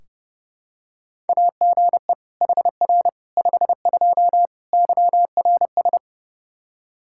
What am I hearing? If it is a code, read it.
Morse code, heard as a keyed tone, about 29 wpm, 700 Hz. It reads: AGE HR 52 YRS
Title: Practice QSO - Exchange 3 - More Info - 30wpm